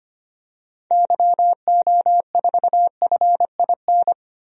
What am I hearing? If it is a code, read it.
Morse code, heard as a keyed tone, about 25 wpm, 700 Hz. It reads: YO4FIN
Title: DXCC-100 Call Signs - 6 Characters - 25wpm